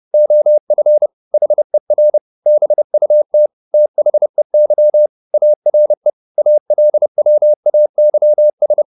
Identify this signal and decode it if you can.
Morse code, heard as a keyed tone, about 30 wpm, 600 Hz. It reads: OF HER BUT THEY ARE ALWAYS
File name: the old man and the sea / the old man and the sea 1 30wpm